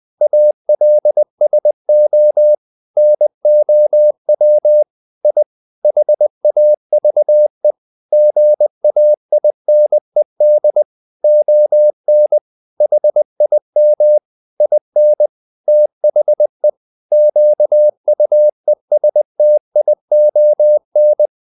Morse code, heard as a keyed tone, about 20 wpm, 600 Hz. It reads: ALSO NOW I HAVE GAINED ON HIM IN THE QUESTION
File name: the old man and the sea part 2 20wpm